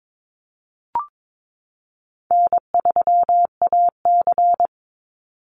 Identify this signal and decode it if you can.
Morse code, heard as a keyed tone, about 22 wpm, 700 Hz. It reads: N3AC
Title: Easy Call Signs Repeated 3x - 22wpm